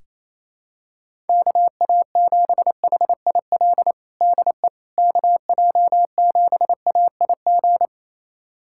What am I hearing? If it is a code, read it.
Morse code, heard as a keyed tone, about 28 wpm, 700 Hz. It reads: KA7HIL DE KJ7AIG
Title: USA Callsign DE Callsign - 6 Characters - 28wpm